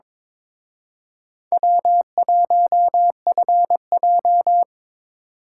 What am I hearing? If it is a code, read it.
Morse code, heard as a keyed tone, about 27 wpm, 700 Hz. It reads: W1FJ
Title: Speed Racing - USA 1x2 Call Signs 22wpm